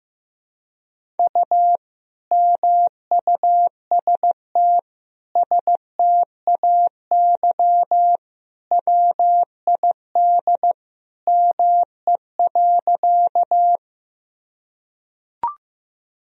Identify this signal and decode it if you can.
Morse code, heard as a keyed tone, about 15 wpm, 700 Hz. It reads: U MUST STAY WID ME.
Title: Sentences from Top 500 Words and Abbreviations 15wpm